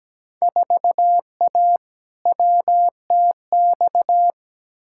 Morse code, heard as a keyed tone, about 17 wpm, 700 Hz. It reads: 4A WTX
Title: Field Day Exchanges - 17wpm